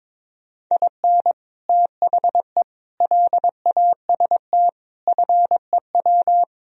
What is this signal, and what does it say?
Morse code, heard as a keyed tone, about 23 wpm, 700 Hz. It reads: IN THE LAST FEW
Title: Speed-Racing - Sets of 5 Words 22wpm